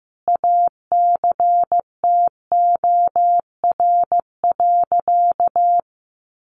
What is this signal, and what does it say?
Morse code, heard as a keyed tone, about 15 wpm, 700 Hz. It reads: ACTOR.
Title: Sentences from Top 1500 Words 15wpm